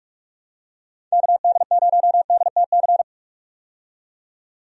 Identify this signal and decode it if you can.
Morse code, heard as a keyed tone, about 42 wpm, 700 Hz. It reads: KD0DTC
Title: USA Call Signs - 6 Characters - 45wpm